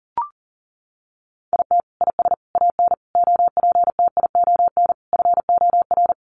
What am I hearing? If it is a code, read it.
Morse code, heard as a keyed tone, about 40 wpm, 700 Hz. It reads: IT IS AN OPTION FOR
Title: Sentences from Top 1300 Words 40wpm